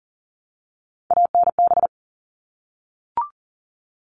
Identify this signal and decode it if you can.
Morse code, heard as a keyed tone, about 40 wpm, 700 Hz. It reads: ANB
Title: All Cut Numbers - 3-Digit 40wpm